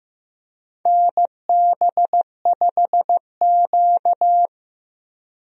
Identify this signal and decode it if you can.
Morse code, heard as a keyed tone, about 15 wpm, 700 Hz. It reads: NB5Q
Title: USA Call Signs - 4 Characters - Repeated 3x - 15wpm